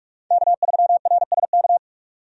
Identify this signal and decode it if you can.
Morse code, heard as a keyed tone, about 43 wpm, 700 Hz. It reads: K3RSK
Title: USA Call Signs - 5 Characters - 45wpm